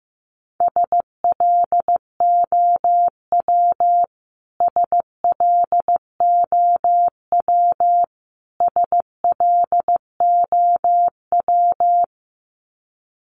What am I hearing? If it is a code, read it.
Morse code, heard as a keyed tone, about 15 wpm, 700 Hz. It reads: SLOW SLOW SLOW
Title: Top 500 Words Repeated 15wpm